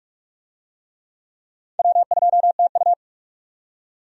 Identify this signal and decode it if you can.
Morse code, heard as a keyed tone, about 46 wpm, 700 Hz. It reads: W2TU
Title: USA Call Signs - 4 Characters - 45wpm